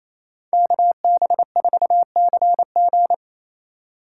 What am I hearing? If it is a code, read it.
Morse code, heard as a keyed tone, about 28 wpm, 700 Hz. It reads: KB4CG
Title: K1USN SST QSO Practice - 28wpm